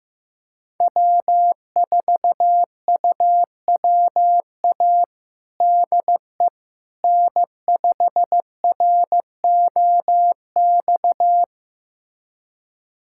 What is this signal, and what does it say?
Morse code, heard as a keyed tone, about 14 wpm, 700 Hz. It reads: W4UWA DE N5ROX
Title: USA Callsign DE Callsign - 5 Characters - 15wpm